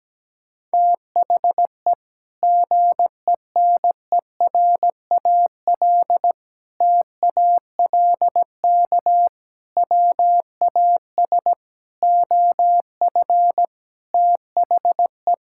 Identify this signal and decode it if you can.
Morse code, heard as a keyed tone, about 17 wpm, 700 Hz. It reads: THE GENERAL TALK WAS OF THE
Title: Sentences from Top 1000 Words - Review Part 5 - 17wpm